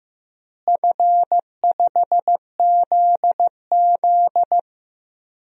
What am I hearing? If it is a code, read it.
Morse code, heard as a keyed tone, about 16 wpm, 700 Hz. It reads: F5ZZ
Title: DXCC-100 Call Signs - 4 Characters - Repeated 3x - 15wpm